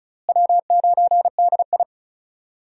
Morse code, heard as a keyed tone, about 36 wpm, 700 Hz. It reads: W9DI
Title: USA Call Signs - 4 Characters - Repeated 3x - 35wpm